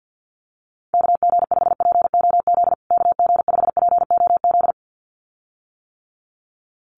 Morse code, heard as a keyed tone, about 50 wpm, 700 Hz. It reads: KG5POZ KG5POZ
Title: USA Call Signs Repeated 2x 50wpm